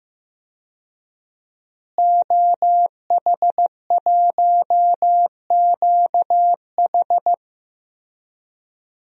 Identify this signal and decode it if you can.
Morse code, heard as a keyed tone, about 15 wpm, 700 Hz. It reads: OH1QH
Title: DXCC-100 Call Signs - 5 Characters - Repeated 3x - 15wpm